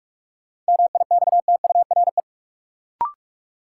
Morse code, heard as a keyed tone, about 44 wpm, 700 Hz. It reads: MIXTURE
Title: Sentences from Top 1500 Words - New Word Drill Alt - 45wpm